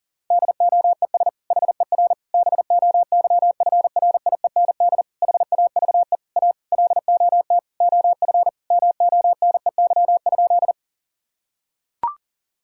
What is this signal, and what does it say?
Morse code, heard as a keyed tone, about 40 wpm, 700 Hz. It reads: DOES HER BOYFRIEND HAVE A LOT OF MONEY?
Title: Sentences from Top 1000 Words - Review Part 5 - 40wpm